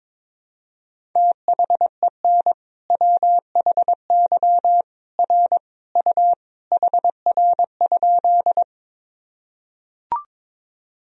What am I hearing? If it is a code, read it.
Morse code, heard as a keyed tone, about 23 wpm, 700 Hz. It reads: THEN WHY R U HR?
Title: Sentences from Top 200 Words and Abbreviations 22wpm